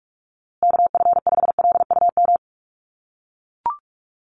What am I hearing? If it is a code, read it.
Morse code, heard as a keyed tone, about 44 wpm, 700 Hz. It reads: KF5LUM
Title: USA Call Signs Repeated 2x 45wpm